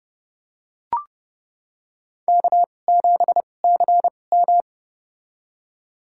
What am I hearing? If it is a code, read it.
Morse code, heard as a keyed tone, about 30 wpm, spaced as for 22 wpm, 700 Hz. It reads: K7CM
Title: USA Call Signs - 4 Characters - 2x Character Spacing - 30wpm